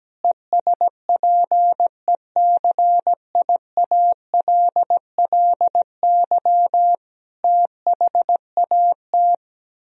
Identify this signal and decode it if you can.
Morse code, heard as a keyed tone, about 17 wpm, 700 Hz. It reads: ESPECIALLY THAT
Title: Sentences from Top 1000 Words - Review Part 4 - 17wpm